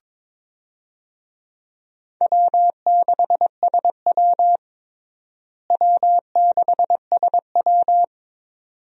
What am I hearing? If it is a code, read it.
Morse code, heard as a keyed tone, about 21 wpm, 700 Hz. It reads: W6SW W6SW
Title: USA Call Signs - 4 Characters - Repeated 2x - 22wpm